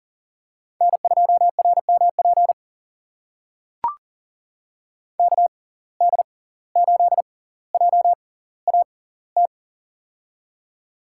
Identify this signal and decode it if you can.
Morse code, heard as a keyed tone, about 40 wpm, spaced as for 19 wpm, 700 Hz. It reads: N2RMP KD8JAT
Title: USA Call Signs - 6x Character Spacing - 40wpm